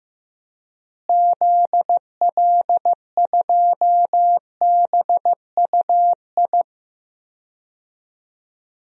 Morse code, heard as a keyed tone, about 15 wpm, 700 Hz. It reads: ZL2BUI
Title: DXCC-100 Call Signs - 6 Characters - Repeated 2x - 15wpm